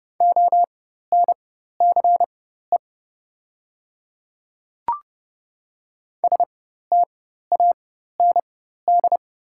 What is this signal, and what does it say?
Morse code, heard as a keyed tone, about 31 wpm, spaced as for 14 wpm, 700 Hz. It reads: ONCE STAND
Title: 4x Character Spacing - Top 500 Words - 30wpm